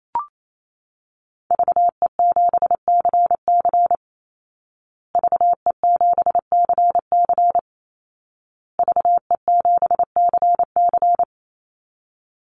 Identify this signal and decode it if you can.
Morse code, heard as a keyed tone, about 28 wpm, 700 Hz. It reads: VE7CC VE7CC VE7CC
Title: Call Signs Repeated 2x - 28wpm